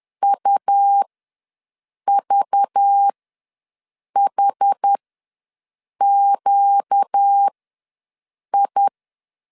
Morse code, heard as a keyed tone, about 10 wpm, spaced as for 6 wpm, 800 Hz. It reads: UVHQI
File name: LEZIONI / 5 LBVC / 5 LBV ws 10wpm 3645